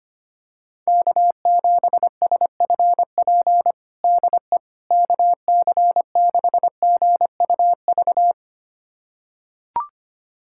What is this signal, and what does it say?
Morse code, heard as a keyed tone, about 25 wpm, 700 Hz. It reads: K7SFP DE KC6GUV
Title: USA Callsign DE Callsign 25wpm